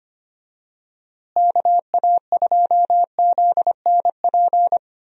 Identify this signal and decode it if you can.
Morse code, heard as a keyed tone, about 25 wpm, 700 Hz. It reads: KA2ZNP
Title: USA Call Signs - 6 Characters - Repeated 2x - 25wpm